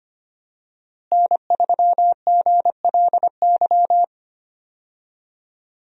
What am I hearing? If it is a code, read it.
Morse code, heard as a keyed tone, about 25 wpm, 700 Hz. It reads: N3GLY
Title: USA Call Signs - 5 Characters - Repeated 3x - 25wpm